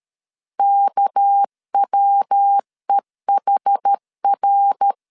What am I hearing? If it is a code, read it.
Morse code, heard as a keyed tone, about 13 wpm, 800 Hz. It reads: KWEHR
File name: LEZIONI / 9 2 12wpm 2301